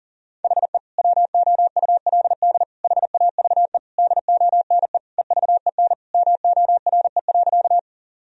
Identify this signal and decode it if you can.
Morse code, heard as a keyed tone, about 40 wpm, 700 Hz. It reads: HE WOULD HAVE DONE EVEN MORE.
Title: Sentences from Top 1000 Words - Review Part 1 - 40wpm